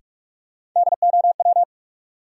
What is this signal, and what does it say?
Morse code, heard as a keyed tone, about 46 wpm, 700 Hz. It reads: DOW
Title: Sets of 5 Words - New Word Drill - 45wpm